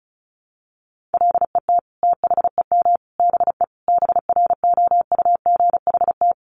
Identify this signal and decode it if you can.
Morse code, heard as a keyed tone, about 34 wpm, 700 Hz. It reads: LET THEM BE BROUGHT
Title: Sentences from Top 1300 Words 35wpm